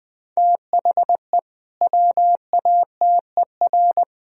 Morse code, heard as a keyed tone, about 20 wpm, 700 Hz. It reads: THE WATER
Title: Sentences from Top 1000 Words - Review Part 3 - 20wpm